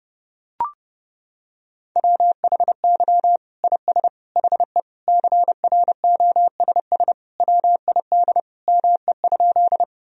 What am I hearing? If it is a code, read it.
Morse code, heard as a keyed tone, about 30 wpm, 700 Hz. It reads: WHY IS HE CROSS WID ME?
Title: Sentences from Top 300 Words and Abbreviations 30wpm